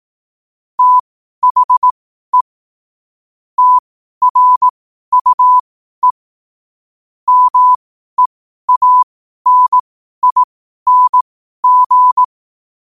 Morse code, heard as a keyed tone, about 18 wpm, spaced as for 12 wpm, 1000 Hz. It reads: THE TRUE MEANING
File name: qotdfiles / 12wpm